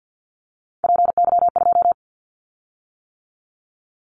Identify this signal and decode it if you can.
Morse code, heard as a keyed tone, about 50 wpm, 700 Hz. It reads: PY2
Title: Callsign Prefixes for Top 50 DXCC - 50wpm